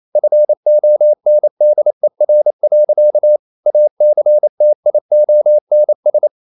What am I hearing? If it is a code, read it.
Morse code, heard as a keyed tone, about 28 wpm, 600 Hz. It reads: FONDER. ACTIONS
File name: CW / Practice / EnglishSayings28